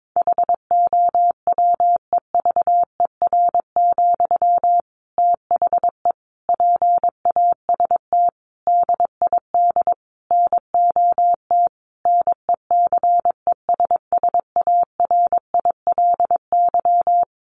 Morse code, heard as a keyed tone, about 22 wpm, 700 Hz. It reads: HOWEVER, THE PAST DID NOT NECESSARILY
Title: Sentences from Top 1400 Words 22wpm